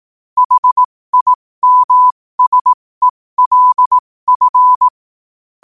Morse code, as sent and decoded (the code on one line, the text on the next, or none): .... .. -- ... .-.. ..-.
HIMSELF